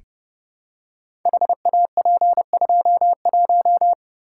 Morse code, heard as a keyed tone, about 30 wpm, 700 Hz. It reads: HAP21
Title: Code Groups - 5 Letter:Numbers - 30wpm